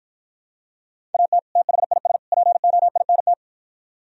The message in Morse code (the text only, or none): AT THIS POINT